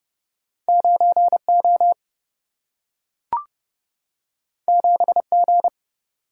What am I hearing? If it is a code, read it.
Morse code, heard as a keyed tone, about 30 wpm, 700 Hz. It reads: 9O 7G
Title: Single-Letter Single-Number 30wpm